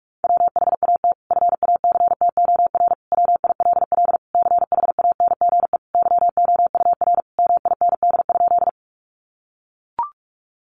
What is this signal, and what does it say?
Morse code, heard as a keyed tone, about 44 wpm, 700 Hz. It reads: WHAT FACTOR WILL CHANGE YOUR MIND?
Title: Sentences from Top 1300 Words 45wpm